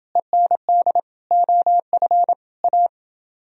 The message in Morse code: END OF A

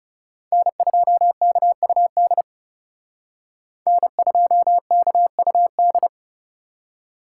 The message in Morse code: N2KUD N2KUD

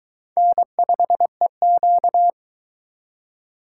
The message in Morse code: N5EQ

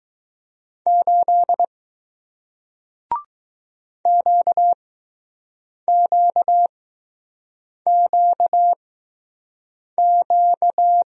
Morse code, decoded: 8 QQQQ

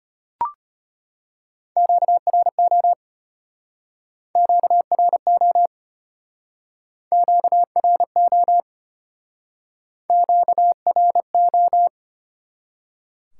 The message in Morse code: QRO QRO QRO QRO

